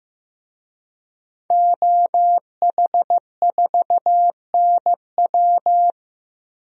Morse code, heard as a keyed tone, about 15 wpm, 700 Hz. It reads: OH4NW
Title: DXCC-100 Call Signs - 5 Characters - Repeated 3x - 15wpm